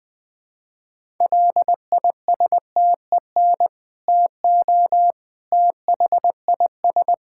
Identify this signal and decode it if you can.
Morse code, heard as a keyed tone, about 21 wpm, 700 Hz. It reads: LISTEN TO THIS
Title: Sentences with 2-Word Phrases - 20wpm